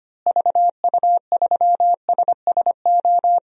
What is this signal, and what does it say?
Morse code, heard as a keyed tone, about 25 wpm, 700 Hz. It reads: VU3SSO
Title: DXCC-100 Call Signs - 6 Characters - Repeated 2x - 25wpm